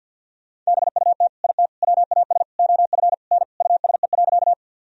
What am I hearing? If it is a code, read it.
Morse code, heard as a keyed tone, about 46 wpm, 700 Hz. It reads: BUT IT WAS OF N USE.
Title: Sentences from Top 100 Words and Abbreviations 50wpm